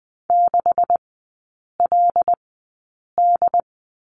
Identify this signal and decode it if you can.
Morse code, heard as a keyed tone, about 20 wpm, spaced as for 8 wpm, 700 Hz. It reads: 6LD
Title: 127 Lesson 21e - Number 6 - Three Characters 20wpm